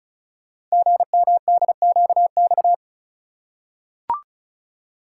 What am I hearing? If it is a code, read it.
Morse code, heard as a keyed tone, about 36 wpm, 700 Hz. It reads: GMDQX
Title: Code Groups - 5 Letters - 35wpm